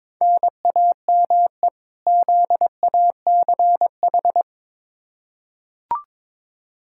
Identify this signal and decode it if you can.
Morse code, heard as a keyed tone, about 22 wpm, 700 Hz. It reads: NAME ZACH
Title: K1USN SST QSO Practice 22wpm